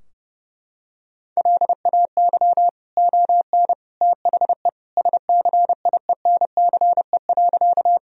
Morse code, heard as a keyed tone, about 30 wpm, 700 Hz. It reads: LAY ON THE SCIENCE.
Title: Sentences from Top 1000 Words - Review Part 2 - 30wpm